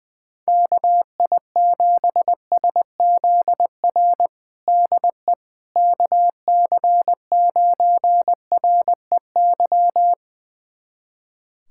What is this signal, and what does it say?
Morse code, heard as a keyed tone, about 21 wpm, 700 Hz. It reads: KI7SZR DE KC9REY
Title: USA Callsign DE Callsign - 6 Characters - 20wpm